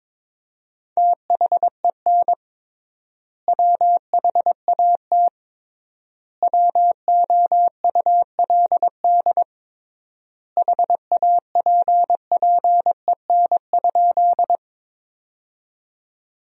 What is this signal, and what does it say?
Morse code, heard as a keyed tone, about 22 wpm, 700 Hz. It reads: THEN WHAT WOULD HAPPEN?
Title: 3x Word-Spacing - Sentences from Top 400 Words - 22wpm